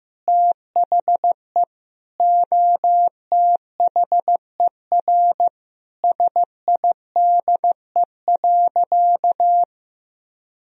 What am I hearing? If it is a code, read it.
Morse code, heard as a keyed tone, about 15 wpm, 700 Hz. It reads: THE OTHER SIDE.
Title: Sentences from Top 1000 Words - Review Part 3 - 15wpm